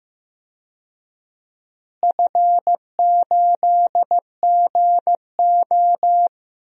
Morse code, heard as a keyed tone, about 15 wpm, 700 Hz. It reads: F8GO
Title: DXCC-100 Call Signs - 4 Characters - Repeated 3x - 15wpm